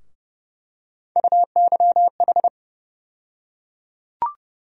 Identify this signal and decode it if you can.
Morse code, heard as a keyed tone, about 30 wpm, 700 Hz. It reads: UYH
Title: USA Call Sign Postfixes - 3 Letters 30wpm